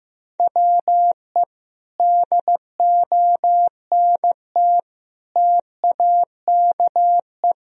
Morse code, heard as a keyed tone, about 15 wpm, 700 Hz. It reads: WE DONT TAKE
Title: Sentences from Top 1000 Words - Review Part 4 - 15wpm